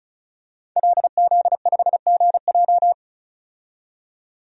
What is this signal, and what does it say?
Morse code, heard as a keyed tone, about 34 wpm, 700 Hz. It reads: LZ5GJ
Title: DXCC-100 Call Signs - 5 Characters - 35wpm